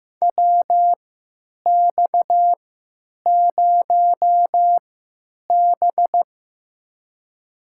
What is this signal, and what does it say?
Morse code, heard as a keyed tone, about 15 wpm, spaced as for 9 wpm, 700 Hz. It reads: WX0B
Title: USA Call Signs - 4 Characters - 3x Character Spacing - 15wpm